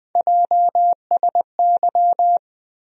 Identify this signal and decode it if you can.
Morse code, heard as a keyed tone, about 20 wpm, 700 Hz. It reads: JSY
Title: USA Call Sign Postfixes - 3 Letters 20wpm